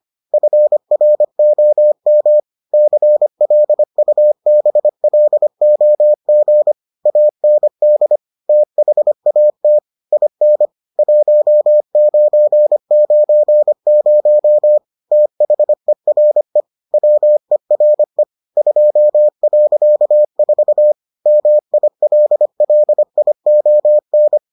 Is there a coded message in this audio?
Yes: Morse code from 0.3 to 24.4 s, heard as a keyed tone, about 25 wpm, 600 Hz. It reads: FROM CLUBLOG AND THAT IN 1990 THERE WERE 2.4 MILLION